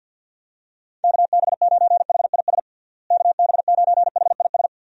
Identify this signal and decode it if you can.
Morse code, heard as a keyed tone, about 52 wpm, 700 Hz. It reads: KB9HIS KB9HIS
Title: USA Call Signs - 6 Characters - Repeated 2x - 50wpm